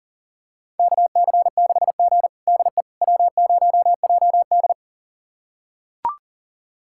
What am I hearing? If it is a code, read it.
Morse code, heard as a keyed tone, about 40 wpm, 700 Hz. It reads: KC6G DE W0JD
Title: USA Callsign DE Callsign - 4 Characters - 40wpm